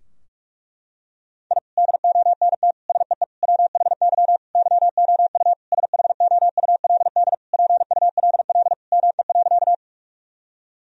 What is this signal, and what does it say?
Morse code, heard as a keyed tone, about 46 wpm, 700 Hz. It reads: I DONT SEE WHY YOU SHOULD PULL ME.
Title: Sentences from Top 1000 Words - Review Part 2 - 45wpm